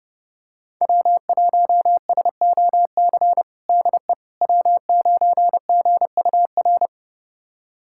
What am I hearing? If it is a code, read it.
Morse code, heard as a keyed tone, about 32 wpm, 700 Hz. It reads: W1SOC DE W9GUR
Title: USA Callsign DE Callsign - 5 Characters - 30wpm